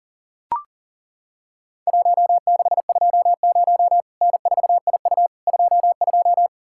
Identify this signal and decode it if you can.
Morse code, heard as a keyed tone, about 40 wpm, 700 Hz. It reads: 1620 N4IU 22
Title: Worked All Europe DX Contest - QTC Exchange - 40wpm